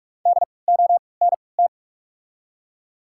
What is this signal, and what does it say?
Morse code, heard as a keyed tone, about 46 wpm, spaced as for 26 wpm, 700 Hz. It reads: DONT
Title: 3x Character Spacing - Top 500 Words - 45wpm